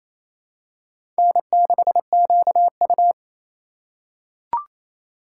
Morse code, heard as a keyed tone, about 28 wpm, 700 Hz. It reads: N6QU
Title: USA Call Signs - 4 Characters - 3x Character Spacing - 28wpm